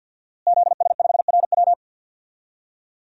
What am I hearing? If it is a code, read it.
Morse code, heard as a keyed tone, about 51 wpm, 700 Hz. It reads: ZS5RW
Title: DXCC-100 Call Signs - 5 Characters - Repeated 3x - 50wpm